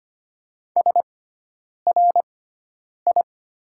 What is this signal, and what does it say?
Morse code, heard as a keyed tone, about 25 wpm, spaced as for 9 wpm, 700 Hz. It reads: SRI
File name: Top 90 QSO Elements - 6x Character Spacing - 25wpm